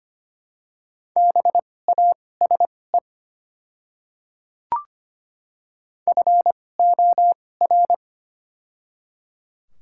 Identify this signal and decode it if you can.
Morse code, heard as a keyed tone, about 25 wpm, spaced as for 18 wpm, 700 Hz. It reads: BASE FOR
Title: 2x Character Spacing - Top 300 Words - 25wpm